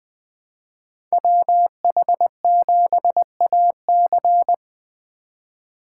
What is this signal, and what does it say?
Morse code, heard as a keyed tone, about 20 wpm, 700 Hz. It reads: WH7AC
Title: Speed Racing - USA Call Signs 20wpm